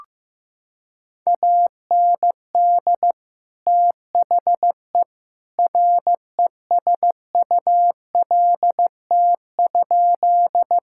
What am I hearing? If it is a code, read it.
Morse code, heard as a keyed tone, about 16 wpm, 700 Hz. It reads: AND THE RESULT?
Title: Sentences from Top 1000 Words - Review Part 5 - 15wpm